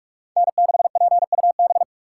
Morse code, heard as a keyed tone, about 40 wpm, 700 Hz. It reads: N6PUB